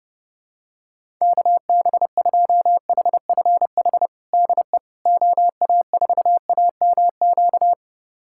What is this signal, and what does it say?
Morse code, heard as a keyed tone, about 30 wpm, 700 Hz. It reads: KB2HFH DE OA4AMQ
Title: USA Callsign DE Callsign - 6 Characters - 30wpm